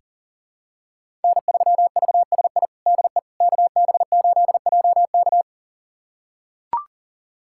Morse code, heard as a keyed tone, about 37 wpm, 700 Hz. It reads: N3VSI DE KB8JK
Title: USA Callsign DE Callsign - 5 Characters - 40wpm